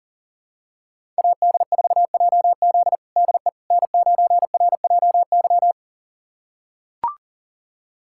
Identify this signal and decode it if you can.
Morse code, heard as a keyed tone, about 37 wpm, 700 Hz. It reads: AD4JZ DE N9RJY
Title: USA Callsign DE Callsign - 5 Characters - 40wpm